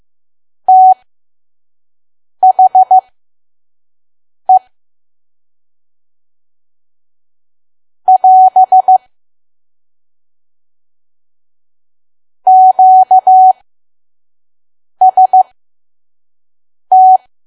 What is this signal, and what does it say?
Morse code, heard as a keyed tone, about 15 wpm, spaced as for 5 wpm, 750 Hz. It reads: THE <AS> QST
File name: Archive / 5 WPM files / PM